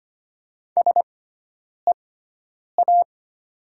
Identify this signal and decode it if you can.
Morse code, heard as a keyed tone, about 25 wpm, spaced as for 9 wpm, 700 Hz. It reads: SEA
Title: 6x Character Spacing - Top 500 Words - 25wpm